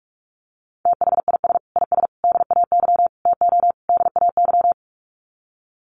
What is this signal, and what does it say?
Morse code, heard as a keyed tone, about 44 wpm, 700 Hz. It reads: THIS IS DAY TO DAY
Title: Sets of 5 Words Encore 45wpm